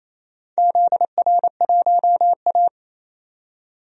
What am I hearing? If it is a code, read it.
Morse code, heard as a keyed tone, about 27 wpm, 700 Hz. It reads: ZR1A
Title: DXCC-100 Call Signs - 4 Characters - 28wpm